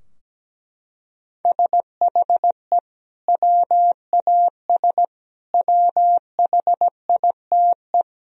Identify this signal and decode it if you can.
Morse code, heard as a keyed tone, about 17 wpm, 700 Hz. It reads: SHE WAS WHITE
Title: Sentences from Top 1000 Words - Review Part 3 - 17wpm